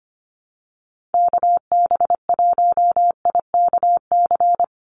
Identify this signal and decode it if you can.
Morse code, heard as a keyed tone, about 25 wpm, 700 Hz. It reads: KB1IKC
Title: USA Call Signs 25wpm